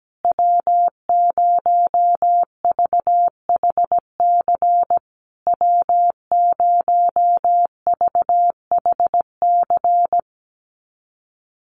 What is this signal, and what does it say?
Morse code, heard as a keyed tone, about 17 wpm, 700 Hz. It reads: W0VHC W0VHC
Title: USA Call Signs Repeated 3x 17wpm